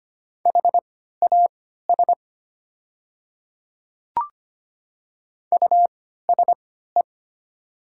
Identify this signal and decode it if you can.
Morse code, heard as a keyed tone, about 25 wpm, spaced as for 14 wpm, 700 Hz. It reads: HAS USE
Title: 3x Character Spacing - Top 100 Words - 25wpm